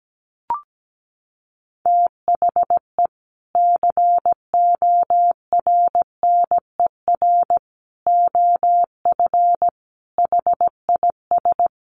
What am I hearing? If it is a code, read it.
Morse code, heard as a keyed tone, about 17 wpm, 700 Hz. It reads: THE CORNER OF HIS
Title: Sets of 5 Words Encore2 17wpm